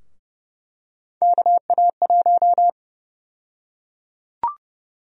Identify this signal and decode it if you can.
Morse code, heard as a keyed tone, about 32 wpm, 700 Hz. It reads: KA1
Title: USA Call Sign Prefixes - Repeated 2x - 30wpm